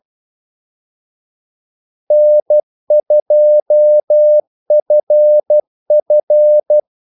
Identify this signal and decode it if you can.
Morse code, heard as a keyed tone, about 12 wpm, 600 Hz. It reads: N2FF
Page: 2 ekstra ordmellemrum mellem kaldesignalerne